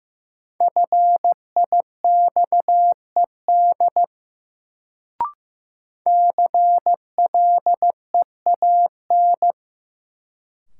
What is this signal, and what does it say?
Morse code, heard as a keyed tone, about 15 wpm, 700 Hz. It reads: FIXED CLEAN
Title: Sentences from Top 1100 Words - New Word Drill Alt - 15wpm